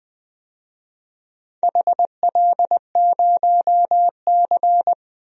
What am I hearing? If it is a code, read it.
Morse code, heard as a keyed tone, about 21 wpm, 700 Hz. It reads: HL0C
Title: DXCC-100 Call Signs - 4 Characters - Repeated 2x - 20wpm